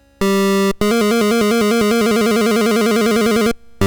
There is no code.